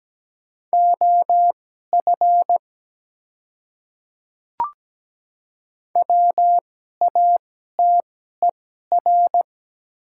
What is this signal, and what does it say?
Morse code, heard as a keyed tone, about 17 wpm, spaced as for 12 wpm, 700 Hz. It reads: OF WATER
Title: 2x Character Spacing - Top 100 Words - 17wpm